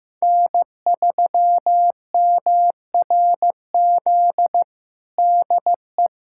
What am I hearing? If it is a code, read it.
Morse code, heard as a keyed tone, about 16 wpm, 700 Hz. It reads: N3MRZ DE
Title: USA Callsign DE Callsign - 5 Characters - 15wpm